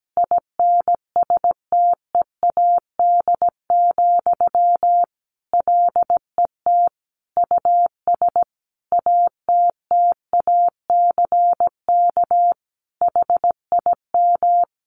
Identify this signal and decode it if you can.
Morse code, heard as a keyed tone, about 17 wpm, 700 Hz. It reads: INSTEAD, LET US ATTACK HIM
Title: Sentences from Top 1500 Words 17wpm